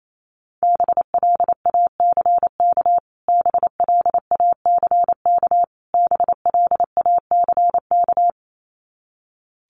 Morse code, heard as a keyed tone, about 28 wpm, 700 Hz. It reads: BLACK BLACK BLACK
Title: Top 500 Words Repeated 28wpm